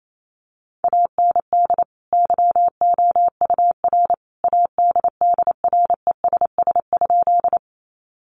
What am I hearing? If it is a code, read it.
Morse code, heard as a keyed tone, about 28 wpm, 700 Hz. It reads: AND YOUR ADDRESS?
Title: Sentences from Top 1000 Words 28wpm